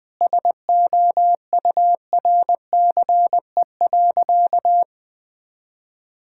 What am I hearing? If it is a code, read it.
Morse code, heard as a keyed tone, about 20 wpm, 700 Hz. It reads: SOURCE.
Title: Sentences from Top 1000 Words - Review Part 5 - 20wpm